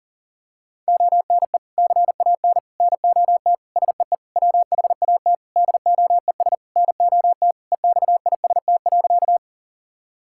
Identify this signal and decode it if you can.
Morse code, heard as a keyed tone, about 40 wpm, 700 Hz. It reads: ONE CAN NOT SEE WHAT DOES NOT EXIST.
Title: Sentences from Top 1000 Words - Review Part 5 - 40wpm